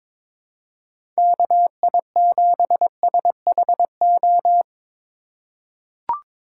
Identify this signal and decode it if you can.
Morse code, heard as a keyed tone, about 22 wpm, 700 Hz. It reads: KI7SHO
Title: USA Call Signs - 6 Characters - 4x Character Spacing - 22wpm